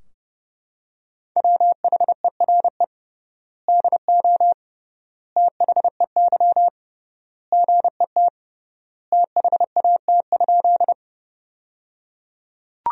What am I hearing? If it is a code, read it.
Morse code, heard as a keyed tone, about 30 wpm, 700 Hz. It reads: WHERE DO THEY GET THAT?
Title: 3x Word-Spacing - Sentences from Top 200 Words - 30wpm